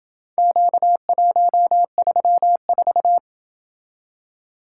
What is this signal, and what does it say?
Morse code, heard as a keyed tone, about 27 wpm, 700 Hz. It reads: Q134